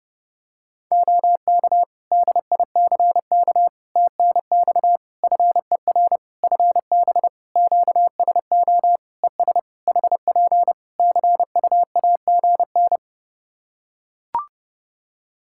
Morse code, heard as a keyed tone, about 32 wpm, 700 Hz. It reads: OK DICK TNX FER FB QSO ES HP CUAGN
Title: Practice QSO - Exchange 4 - Ending - 30wpm